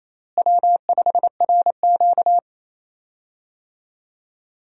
Morse code, heard as a keyed tone, about 28 wpm, 700 Hz. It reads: W5RQ
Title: USA Call Signs - 4 Characters - Repeated 2x - 28wpm